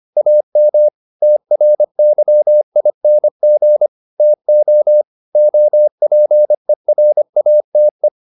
Morse code, heard as a keyed tone, about 25 wpm, 600 Hz. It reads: AM TRYING TO OPERATE